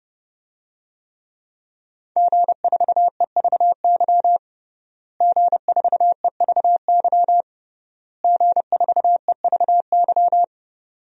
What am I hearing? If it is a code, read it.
Morse code, heard as a keyed tone, about 30 wpm, 700 Hz. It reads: G4EVY G4EVY G4EVY
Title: DXCC-100 Call Signs - 5 Characters - Repeated 3x - 30wpm